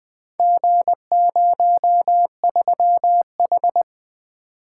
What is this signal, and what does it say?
Morse code, heard as a keyed tone, about 20 wpm, 700 Hz. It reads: G03H
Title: Four Letter-Number Combinations - 20wpm